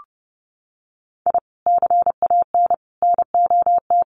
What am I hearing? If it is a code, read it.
Morse code, heard as a keyed tone, about 30 wpm, 700 Hz. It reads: I CAN NOT
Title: Sets of 5 Words Encore2 30wpm